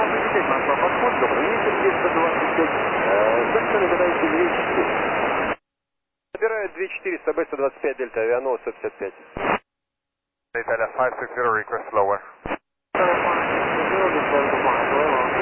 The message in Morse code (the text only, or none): none